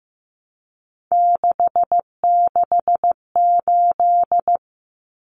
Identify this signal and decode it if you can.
Morse code, heard as a keyed tone, about 15 wpm, 700 Hz. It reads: 668
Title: Numbers - 3-Digit 15wpm